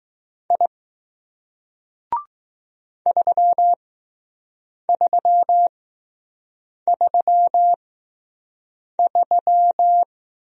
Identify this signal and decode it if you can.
Morse code, heard as a keyed tone, about 20 wpm, spaced as for 6 wpm, 700 Hz. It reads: I 3333